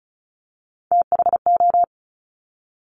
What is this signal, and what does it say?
Morse code, heard as a keyed tone, about 36 wpm, 700 Hz. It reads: THO
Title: Top Three Letter Combinations in Words 35wpm